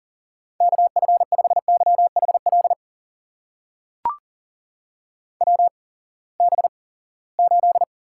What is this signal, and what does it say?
Morse code, heard as a keyed tone, about 40 wpm, spaced as for 23 wpm, 700 Hz. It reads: KF5YHL WB8
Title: USA Call Signs - 6 Characters - 8x Character Spacing - 40wpm